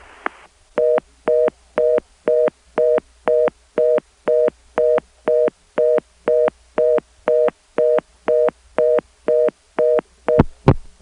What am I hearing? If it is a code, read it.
Reorder tone (North America), 480 + 620 Hz.